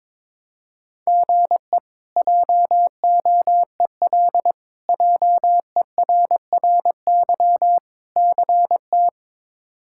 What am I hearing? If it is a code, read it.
Morse code, heard as a keyed tone, about 22 wpm, 700 Hz. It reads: GE JOEL JERRY CT
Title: K1USN SST QSO Practice - 22wpm